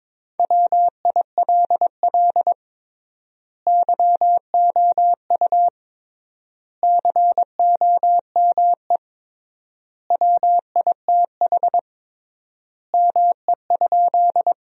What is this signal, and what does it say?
Morse code, heard as a keyed tone, about 22 wpm, 700 Hz. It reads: WILL YOU COME WITH ME?
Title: 3x Word-Spacing - Sentences from Top 200 Words - 22wpm